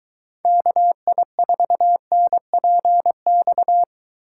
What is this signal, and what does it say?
Morse code, heard as a keyed tone, about 21 wpm, 700 Hz. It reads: KI4NPX